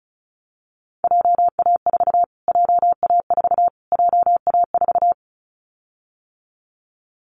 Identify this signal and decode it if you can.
Morse code, heard as a keyed tone, about 34 wpm, 700 Hz. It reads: JA4 JA4 JA4
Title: Callsign Prefixes for Top 25 DXCC - Repeated 3x - 35wpm